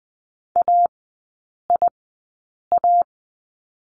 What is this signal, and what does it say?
Morse code, heard as a keyed tone, about 20 wpm, spaced as for 8 wpm, 700 Hz. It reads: AIA
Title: 035 Lesson 7e - Letter s - Three Characters 20wpm